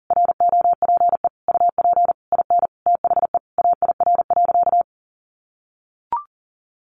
Morse code, heard as a keyed tone, about 40 wpm, 700 Hz. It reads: ROPE UP IN THE AIR.
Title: Easy Phrases 40wpm